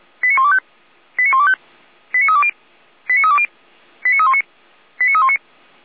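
Five-tone selective call (ZVEI), heard as quick runs of five tones, tones of about 70 ms. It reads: 89125 89125 89320 89320 89319 89319
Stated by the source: ZVEI five-tone selective call received by SDR.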